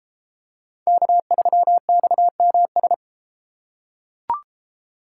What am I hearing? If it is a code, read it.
Morse code, heard as a keyed tone, about 33 wpm, 700 Hz. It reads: K3XMS